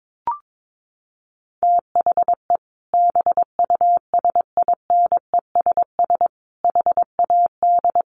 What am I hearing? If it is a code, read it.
Morse code, heard as a keyed tone, about 22 wpm, 700 Hz. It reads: THE BUSINESS HAD
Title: Sentences from Top 1400 Words 22wpm